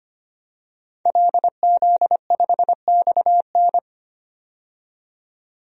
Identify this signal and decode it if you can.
Morse code, heard as a keyed tone, about 25 wpm, 700 Hz. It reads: LZ5XN
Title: DXCC-100 Call Signs - 5 Characters - 25wpm